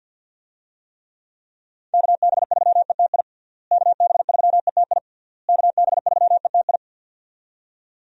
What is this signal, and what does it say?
Morse code, heard as a keyed tone, about 52 wpm, 700 Hz. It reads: KB3ETI KB3ETI KB3ETI
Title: USA Call Signs - 6 Characters - Repeated 3x - 50wpm